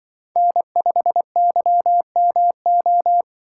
Morse code, heard as a keyed tone, about 23 wpm, 700 Hz. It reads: N5YMO
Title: Speed Racing - USA 1x3 Call Signs 20wpm